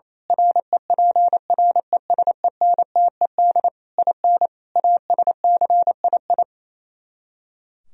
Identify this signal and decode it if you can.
Morse code, heard as a keyed tone, about 26 wpm, 700 Hz. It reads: REPRESENTED IN ASCII
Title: Speed-Racing - Sets of 5 Words 28wpm